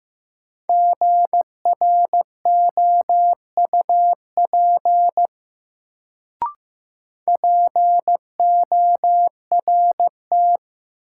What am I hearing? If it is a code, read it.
Morse code, heard as a keyed tone, about 15 wpm, 700 Hz. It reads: GROUP PORT
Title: Sentences from Top 1000 Words - Review Part 1 - New Word Drill - 15wpm